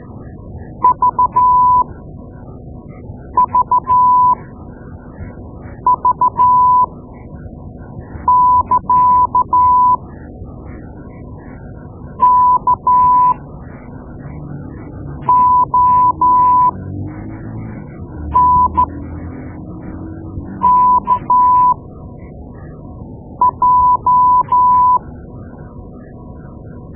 Morse code, heard as a keyed tone, about 12 wpm, spaced as for 4 wpm, 990 Hz. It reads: VVV <KA> KONKJ
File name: Morse Lessons / Morse Lessons / Tape-2A